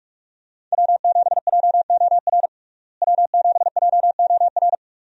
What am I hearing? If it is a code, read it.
Morse code, heard as a keyed tone, about 46 wpm, 700 Hz. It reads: W7JOR W7JOR
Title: USA Call Signs - 5 Characters - Repeated 2x - 45wpm